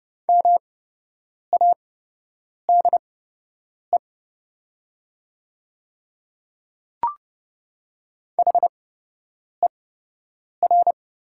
Morse code, heard as a keyed tone, about 30 wpm, spaced as for 8 wpm, 700 Hz. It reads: MADE HER
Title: 8x Character Spacing - Top 200 Words - 30wpm